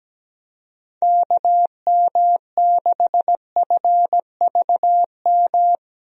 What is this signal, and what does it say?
Morse code, heard as a keyed tone, about 17 wpm, 700 Hz. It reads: KM6FVM
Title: USA Call Signs - 6 Characters - Repeated 2x - 17wpm